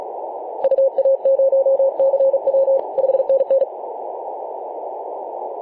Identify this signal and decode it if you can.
Morse code, heard as a keyed tone, about 40 wpm, 540 Hz, with noise. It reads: UA0CW 5NN
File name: 2013/Sep/22 0824z XZ1Z 24.896MHz CW